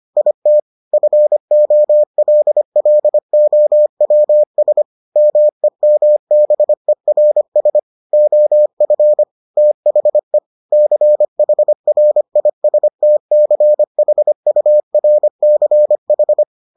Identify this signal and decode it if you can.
Morse code, heard as a keyed tone, about 25 wpm, 600 Hz. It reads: IT FOLLOWS MEMBERS OF THE CHRISTCHURCH